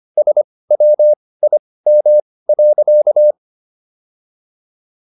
Morse code, heard as a keyed tone, about 25 wpm, spaced as for 18 wpm, 600 Hz. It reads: SWIM.
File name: SS108 18